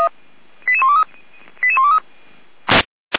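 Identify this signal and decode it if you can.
Five-tone selective call (ZVEI), heard as quick runs of five tones, tones of about 80 ms. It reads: 88123 88123
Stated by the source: ZVEI five-tone selective call received on an SDR.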